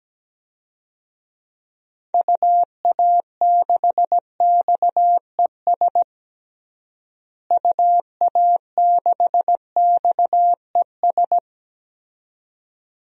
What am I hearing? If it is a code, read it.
Morse code, heard as a keyed tone, about 17 wpm, 700 Hz. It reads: UA6XES UA6XES
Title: DXCC-100 Call Signs - 6 Characters - Repeated 2x - 17wpm